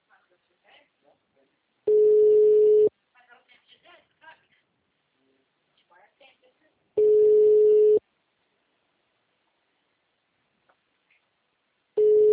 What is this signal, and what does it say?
Ringback tone (Europe, 425 Hz), with speech.